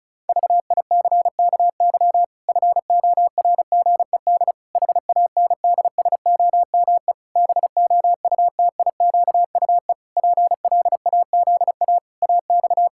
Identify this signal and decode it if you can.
Morse code, heard as a keyed tone, about 34 wpm, 700 Hz. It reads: VICKY FORGED HANDSOME BOUTIQUE PLAZA AX
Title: Pangram Challenge v2 - 35wpm